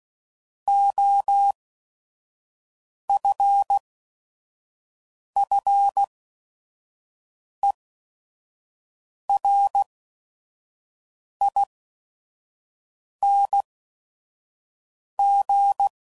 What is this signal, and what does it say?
Morse code, heard as a keyed tone, about 16 wpm, spaced as for 5 wpm, 800 Hz. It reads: OFFERING
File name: Suffixes / Special ING's / VerySlowToFaster ING's